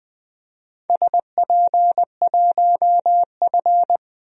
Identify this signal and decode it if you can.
Morse code, heard as a keyed tone, about 20 wpm, 700 Hz. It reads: SP1F
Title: DXCC-100 Call Signs - 4 Characters - 20wpm